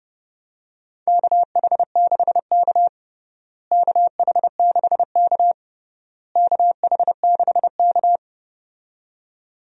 Morse code, heard as a keyed tone, about 30 wpm, 700 Hz. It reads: KH6K KH6K KH6K
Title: DXCC-100 Call Signs - 4 Characters - Repeated 3x - 30wpm